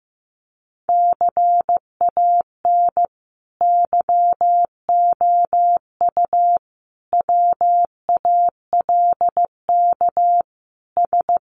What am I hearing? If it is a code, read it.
Morse code, heard as a keyed tone, about 15 wpm, 700 Hz. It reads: CAN YOU WALK S
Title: Sentences from Top 1000 Words 15wpm